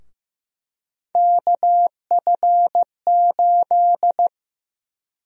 Morse code, heard as a keyed tone, about 15 wpm, 700 Hz. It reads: KF8
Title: USA Call Sign Prefixes - 15wpm